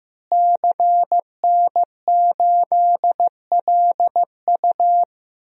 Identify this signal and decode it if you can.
Morse code, heard as a keyed tone, about 15 wpm, 700 Hz. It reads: CN8LU
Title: DXCC-100 Call Signs - 5 Characters - Repeated 2x - 15wpm